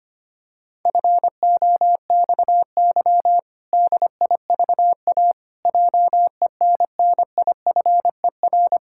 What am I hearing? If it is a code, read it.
Morse code, heard as a keyed tone, about 25 wpm, 700 Hz. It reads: FOXY DIVA JENNIFER
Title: Pangram Challenge v2 - 25wpm